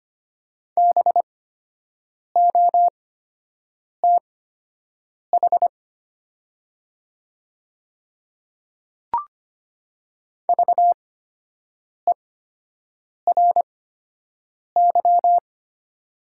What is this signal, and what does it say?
Morse code, heard as a keyed tone, about 25 wpm, spaced as for 7 wpm, 700 Hz. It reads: BOTH VERY
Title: 8x Character Spacing - Top 500 Words - 25wpm